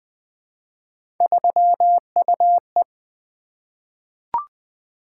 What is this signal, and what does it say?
Morse code, heard as a keyed tone, about 20 wpm, 700 Hz. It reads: 3UE
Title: Three Letter-Number Combinations - 20wpm